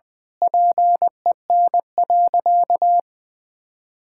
Morse code, heard as a keyed tone, about 20 wpm, 700 Hz. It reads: PEN.
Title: Pangram Challenge v2 - 20wpm